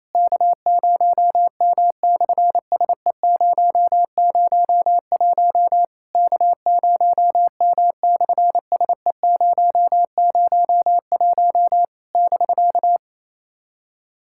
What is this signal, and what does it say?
Morse code, heard as a keyed tone, about 28 wpm, 700 Hz. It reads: K0M/SE001 K0M/SE001 <BK>
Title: SOTA QSO Practice 28wpm